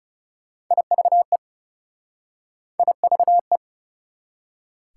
Morse code, heard as a keyed tone, about 38 wpm, 690 Hz. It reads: IVE IVE